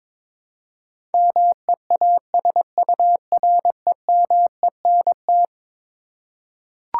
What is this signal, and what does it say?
Morse code, heard as a keyed tone, about 22 wpm, 700 Hz. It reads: MEASUREMENT
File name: practice / 22wpm / Eleven Letter Words - 22wpm